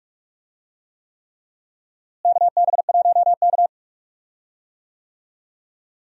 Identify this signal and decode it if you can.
Morse code, heard as a keyed tone, about 41 wpm, 700 Hz. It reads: KB1K
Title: USA Call Signs - 4 Characters - Repeated 3x - 45wpm